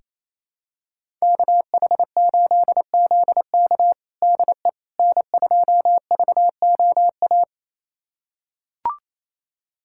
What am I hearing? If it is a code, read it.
Morse code, heard as a keyed tone, about 28 wpm, 700 Hz. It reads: KH8ZK DE N2VOA
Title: USA Callsign DE Callsign - 5 Characters - 28wpm